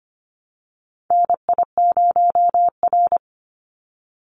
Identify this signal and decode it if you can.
Morse code, heard as a keyed tone, about 25 wpm, 700 Hz. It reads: NI0R
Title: Call Signs - 25wpm